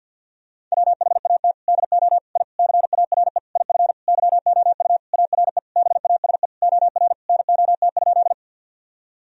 Morse code, heard as a keyed tone, about 52 wpm, 700 Hz. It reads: WHAT DO I CARE IF YOU ARE BASE OR NOT?
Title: Sentences from Top 1000 Words - Review Part 1 - 50wpm